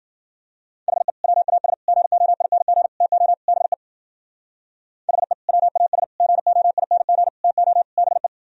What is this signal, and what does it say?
Morse code, heard as a keyed tone, about 51 wpm, 700 Hz. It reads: HE WAS GOING TO BE HE WAS GOING TO BE